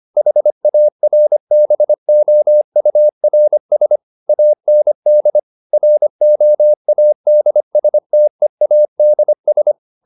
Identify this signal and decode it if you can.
Morse code, heard as a keyed tone, about 25 wpm, 600 Hz. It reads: HARBOURS AND ROADSTEADS
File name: the old man and the sea part 1 25wpm